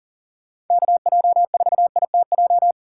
Morse code, heard as a keyed tone, about 51 wpm, 700 Hz. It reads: KJ4ITJ